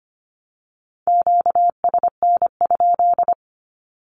Morse code, heard as a keyed tone, about 25 wpm, 700 Hz. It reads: QSN?
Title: Q-Codes Repeated 3x 25wpm